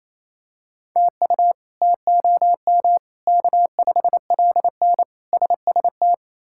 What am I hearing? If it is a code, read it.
Morse code, heard as a keyed tone, about 28 wpm, 700 Hz. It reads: TU TOM K5LN SST
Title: K1USN SST QSO Practice - 28wpm